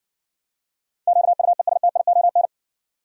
Morse code, heard as a keyed tone, about 54 wpm, 700 Hz. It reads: QUESTION